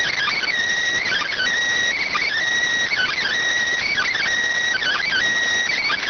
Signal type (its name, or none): Emergency Alert System (SAME) tones